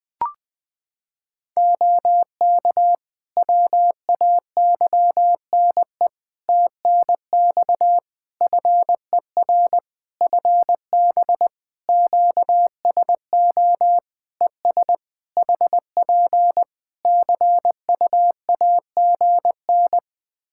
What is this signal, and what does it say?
Morse code, heard as a keyed tone, about 20 wpm, 700 Hz. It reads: OK WAYNE TNX FER FB QSO ES HP CUAGN
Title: Practice QSO - Exchange 4 - Ending - 20wpm